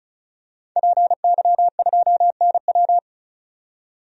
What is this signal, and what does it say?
Morse code, heard as a keyed tone, about 36 wpm, 700 Hz. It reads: PY2NW
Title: DXCC-100 Call Signs - 5 Characters - 35wpm